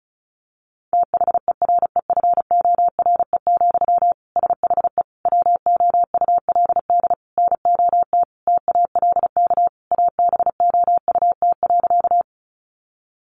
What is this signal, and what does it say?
Morse code, heard as a keyed tone, about 34 wpm, 700 Hz. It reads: THEREFORE, SHE WOULD NOT TALK ABOUT.
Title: Sentences from Top 800 Words 35wpm